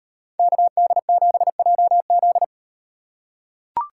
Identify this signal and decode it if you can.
Morse code, heard as a keyed tone, about 37 wpm, 700 Hz. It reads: KD7JZ